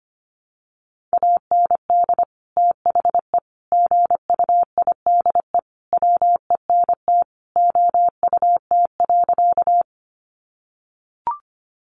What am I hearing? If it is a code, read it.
Morse code, heard as a keyed tone, about 25 wpm, 700 Hz. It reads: AND THE GUIDE WENT OUT.
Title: Sentences from Top 600 Words 25wpm